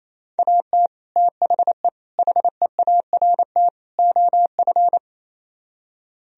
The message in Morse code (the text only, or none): AT THE HEART OF